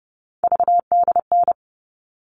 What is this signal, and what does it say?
Morse code, heard as a keyed tone, about 30 wpm, 700 Hz. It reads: VDN
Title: All Cut Numbers - 3-Digit 30wpm